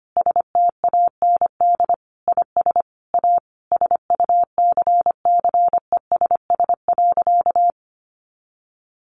Morse code, heard as a keyed tone, about 25 wpm, 700 Hz. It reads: STAND IS A SUCCESS.
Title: Additional Easy Phrases 25wpm